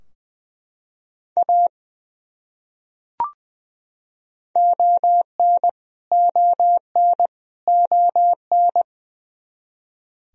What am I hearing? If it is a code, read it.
Morse code, heard as a keyed tone, about 20 wpm, 700 Hz. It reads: A ON ON ON
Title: Top 100 Words Repeated 20wpm